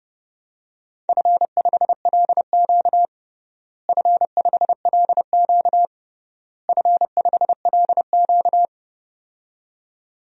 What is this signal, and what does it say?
Morse code, heard as a keyed tone, about 32 wpm, 700 Hz. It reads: F5LQ F5LQ F5LQ
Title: DXCC-100 Call Signs - 4 Characters - Repeated 3x - 30wpm